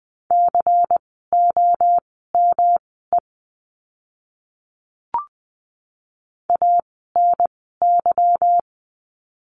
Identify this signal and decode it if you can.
Morse code, heard as a keyed tone, about 20 wpm, spaced as for 14 wpm, 700 Hz. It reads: COME ANY
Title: 146 Lesson 24f - Letter y - Words 20wpm